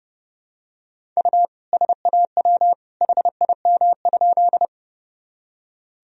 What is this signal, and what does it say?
Morse code, heard as a keyed tone, about 32 wpm, 700 Hz. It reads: U SAW HIM?
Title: Sentences from Top 300 Words and Abbreviations 30wpm